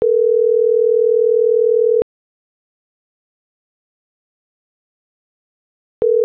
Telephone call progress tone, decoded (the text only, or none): ringback tone